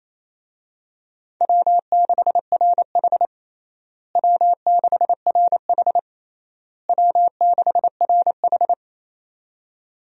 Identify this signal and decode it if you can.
Morse code, heard as a keyed tone, about 28 wpm, 700 Hz. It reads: W6RH W6RH W6RH
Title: USA Call Signs - 4 Characters - Repeated 3x - 28wpm